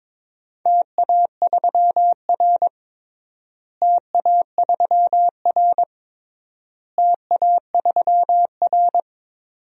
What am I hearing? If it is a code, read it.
Morse code, heard as a keyed tone, about 21 wpm, 700 Hz. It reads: TA3R TA3R TA3R
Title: DXCC-100 Call Signs - 4 Characters - Repeated 3x - 22wpm